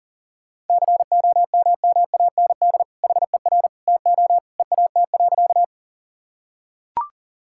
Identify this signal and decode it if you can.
Morse code, heard as a keyed tone, about 40 wpm, 700 Hz. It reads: COMMAND HER TO EAT.
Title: Sentences from Top 1600 Words 40wpm